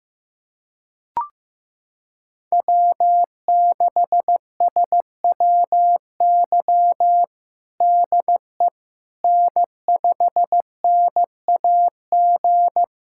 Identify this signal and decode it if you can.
Morse code, heard as a keyed tone, about 15 wpm, 700 Hz. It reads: W6SWY DE N5NAG
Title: USA Callsign DE Callsign 15wpm